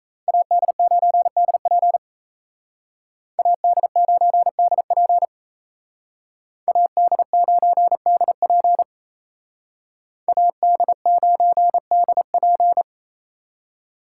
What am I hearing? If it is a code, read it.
Morse code, heard as a keyed tone, about 35 wpm, 700 Hz. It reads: AD9DP AD9DP AD9DP AD9DP